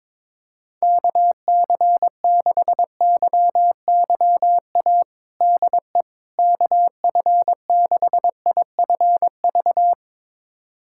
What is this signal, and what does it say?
Morse code, heard as a keyed tone, about 22 wpm, 700 Hz. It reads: KC6YYA DE KF6IFV
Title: USA Callsign DE Callsign - 6 Characters - 22wpm